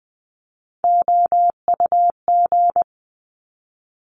Morse code, heard as a keyed tone, about 19 wpm, 700 Hz. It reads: OUG
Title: Top Three Letter Combinations in Words 20wpm